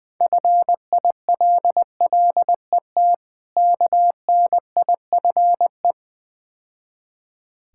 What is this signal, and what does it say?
Morse code, heard as a keyed tone, about 20 wpm, 700 Hz. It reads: FILLET KNIFE
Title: Difficult Words - Speed-Racing with Hints - 20wpm